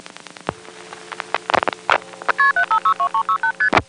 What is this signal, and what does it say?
Multi-frequency trunk signalling, heard as fast pairs of tones, tones of about 90 ms. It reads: KP7562368ST
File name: phonetrips / EvanNX11HQ